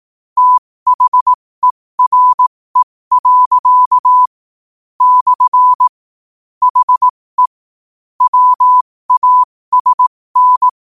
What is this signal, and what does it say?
Morse code, heard as a keyed tone, about 18 wpm, spaced as for 15 wpm, 1000 Hz. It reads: THERE. / HE WASN
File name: qotdfiles / 15wpm